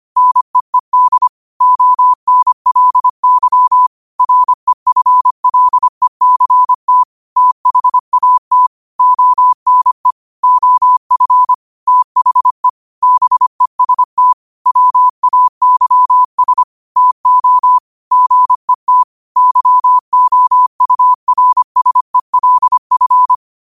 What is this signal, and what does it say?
Morse code, heard as a keyed tone, about 25 wpm, 1000 Hz. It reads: NEED ONLY REFLECT THAT ONE OF THE BEST WAYS TO GET YOURSELF
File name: qotdfiles / 25wpm